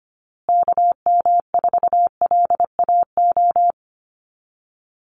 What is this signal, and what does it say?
Morse code, heard as a keyed tone, about 25 wpm, 700 Hz. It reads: KM4LAO
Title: Call Signs Repeated 3x - 25wpm